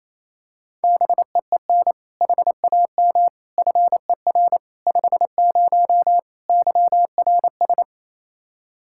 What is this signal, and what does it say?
Morse code, heard as a keyed tone, about 28 wpm, 700 Hz. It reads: BEEN HAM FER 50 YRS
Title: Practice QSO - Exchange 3 - More Info - 28wpm